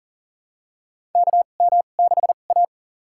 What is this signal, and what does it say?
Morse code, heard as a keyed tone, about 40 wpm, spaced as for 29 wpm, 690 Hz. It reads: KM6A
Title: USA Call Signs - 2x Character Spacing - 40wpm